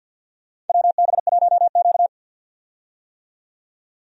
Morse code, heard as a keyed tone, about 53 wpm, 700 Hz. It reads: WB1Q
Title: USA Call Signs - 4 Characters - 50wpm